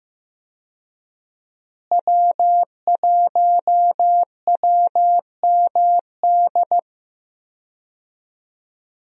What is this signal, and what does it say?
Morse code, heard as a keyed tone, about 15 wpm, 700 Hz. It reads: W1WMD
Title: USA Call Signs - 5 Characters - Repeated 3x - 15wpm